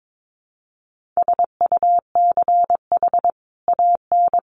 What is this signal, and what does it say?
Morse code, heard as a keyed tone, about 22 wpm, 700 Hz. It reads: SUCH AN
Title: Sentences from Top 1100 Words 22wpm